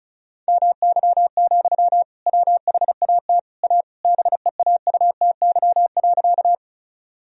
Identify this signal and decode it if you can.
Morse code, heard as a keyed tone, about 34 wpm, 700 Hz. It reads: MY, WHAT A BEAUTY.
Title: Sentences from Top 1000 Words - Review Part 2 - 35wpm